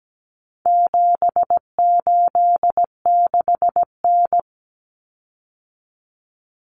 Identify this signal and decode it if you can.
Morse code, heard as a keyed tone, about 17 wpm, 700 Hz. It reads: 786N
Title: Common Cut Numbers - 4-Digit 17wpm